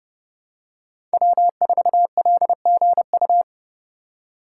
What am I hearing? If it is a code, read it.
Morse code, heard as a keyed tone, about 28 wpm, 700 Hz. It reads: W4LGU
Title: Code Groups - 5 Letter:Numbers - 30wpm